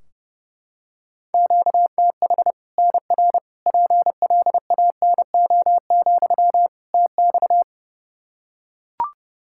Morse code, heard as a keyed tone, about 32 wpm, 700 Hz. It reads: QTH NR PLANO, TX
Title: Practice QSO - Exchange 1 - Beginning - 30wpm